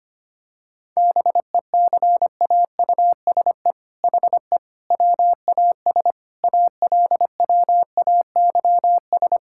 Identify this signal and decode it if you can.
Morse code, heard as a keyed tone, about 25 wpm, 700 Hz. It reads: BECAUSE HE WAS ALWAYS
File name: practice / 25wpm / All-Age Appropriate Jokes - 25wpm